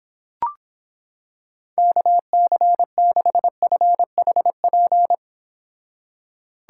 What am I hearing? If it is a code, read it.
Morse code, heard as a keyed tone, about 26 wpm, 700 Hz. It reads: KC6FHP